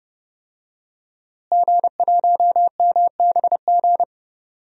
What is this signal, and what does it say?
Morse code, heard as a keyed tone, about 30 wpm, 700 Hz. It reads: G1MBG
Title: DXCC-100 Call Signs - 5 Characters - Repeated 3x - 30wpm